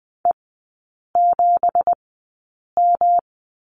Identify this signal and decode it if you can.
Morse code, heard as a keyed tone, about 20 wpm, spaced as for 8 wpm, 700 Hz. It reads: E7M
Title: 212 Lesson 34e - Number 8 - Three Characters 20wpm